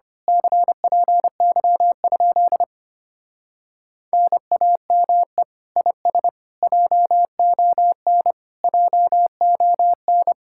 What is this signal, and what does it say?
Morse code, heard as a keyed tone, about 24 wpm, 700 Hz. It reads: CPY? NAME IS JON JON
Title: Speed-Racing Common QSO Phrases - 25wpm